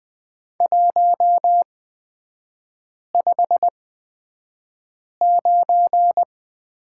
Morse code, heard as a keyed tone, about 20 wpm, spaced as for 5 wpm, 690 Hz. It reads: 159
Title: 255 Lesson 41e - Period - Three Characters 20wpm